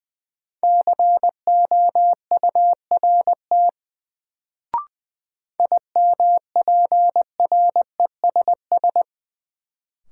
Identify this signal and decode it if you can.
Morse code, heard as a keyed tone, about 20 wpm, 700 Hz. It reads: COURT IMPRESS
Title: Sentences from Top 1600 Words - New Word Drill - 20wpm